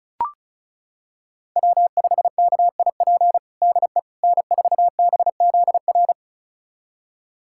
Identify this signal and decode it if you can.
Morse code, heard as a keyed tone, about 34 wpm, 700 Hz. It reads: W5KIP DE N4BZR
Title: USA Callsign DE Callsign - 5 Characters - 35wpm